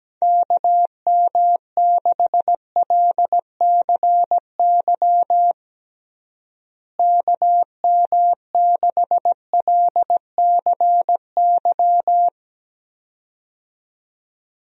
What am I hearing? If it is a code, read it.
Morse code, heard as a keyed tone, about 17 wpm, 700 Hz. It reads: KM6LCY KM6LCY
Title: USA Call Signs - 6 Characters - Repeated 2x - 17wpm